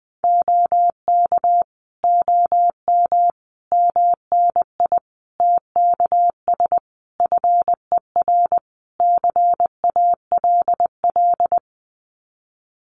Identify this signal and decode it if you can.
Morse code, heard as a keyed tone, about 20 wpm, 700 Hz. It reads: OK OM MNI TKS FER CALL
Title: Common QSO Phrases 20wpm